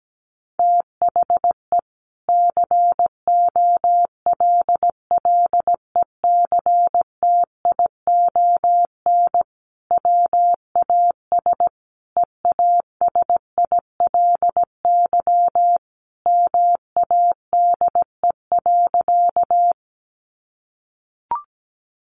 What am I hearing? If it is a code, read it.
Morse code, heard as a keyed tone, about 17 wpm, 700 Hz. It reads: THE COLLECTION WAS EASILY MADE.
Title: Sentences from Top 1200 Words 17wpm